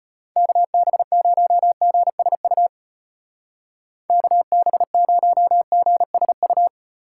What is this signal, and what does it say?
Morse code, heard as a keyed tone, about 30 wpm, 700 Hz. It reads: KB0GSU KB0GSU
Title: Speed Racing - USA 2x3 Call Signs 25wpm